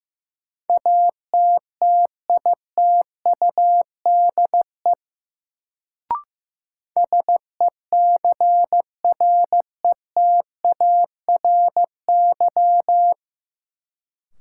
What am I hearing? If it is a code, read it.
Morse code, heard as a keyed tone, about 16 wpm, 700 Hz. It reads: ATTITUDE SECRETARY
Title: Sentences from Top 1300 Words - New Word Drill - 15wpm